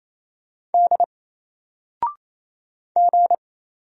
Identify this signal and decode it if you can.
Morse code, heard as a keyed tone, about 27 wpm, 700 Hz. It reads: D G
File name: Single-Letter-Number-28wpm